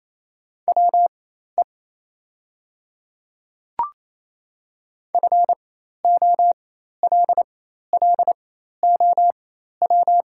Morse code, heard as a keyed tone, about 28 wpm, spaced as for 13 wpm, 700 Hz. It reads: WE FOLLOW
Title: 4x Character Spacing - Top 200 Words - 28wpm